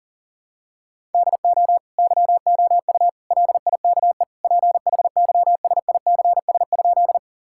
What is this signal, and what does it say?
Morse code, heard as a keyed tone, about 40 wpm, 700 Hz. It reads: DO YOU LIKE PHYSICS?
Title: Sentences from Top 1000 Words - Review Part 5 - 40wpm